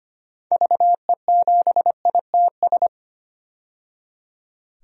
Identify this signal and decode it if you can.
Morse code, heard as a keyed tone, about 25 wpm, 700 Hz. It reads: VE7ITS
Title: DXCC-100 Call Signs - 6 Characters - 25wpm